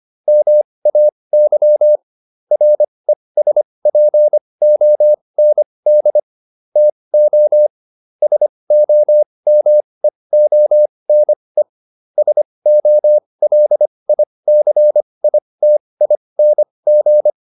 Morse code, heard as a keyed tone, about 25 wpm, spaced as for 20 wpm, 600 Hz. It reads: MAY RESPOND TO SOMEONE SOLICITING